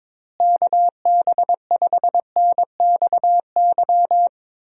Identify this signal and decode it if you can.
Morse code, heard as a keyed tone, about 22 wpm, 700 Hz. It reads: KB5NXY
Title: USA Call Signs - 6 Characters - Repeated 3x - 22wpm